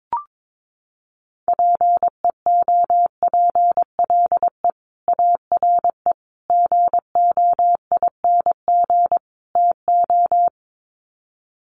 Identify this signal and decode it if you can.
Morse code, heard as a keyed tone, about 22 wpm, 700 Hz. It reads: PEOPLE ARE GOING TO
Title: Sets of 4 Words 22wpm